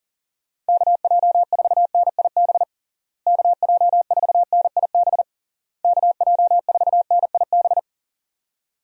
Morse code, heard as a keyed tone, about 40 wpm, 700 Hz. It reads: KJ4NIB KJ4NIB KJ4NIB
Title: USA Call Signs - 6 Characters - Repeated 3x - 40wpm